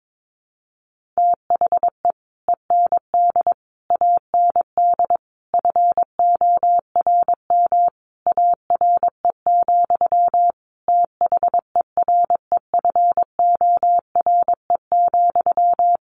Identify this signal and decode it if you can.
Morse code, heard as a keyed tone, about 22 wpm, 700 Hz. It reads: THE END AND FORM ARE, THEREFORE,
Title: Sentences from Top 1400 Words 22wpm